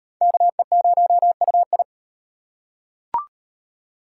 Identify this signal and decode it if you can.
Morse code, heard as a keyed tone, about 37 wpm, 700 Hz. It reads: KE0UI